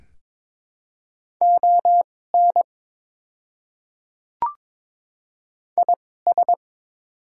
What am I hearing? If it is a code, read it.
Morse code, heard as a keyed tone, about 22 wpm, spaced as for 16 wpm, 700 Hz. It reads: ON IS
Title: 2x Character Spacing - Top 100 Words - 22wpm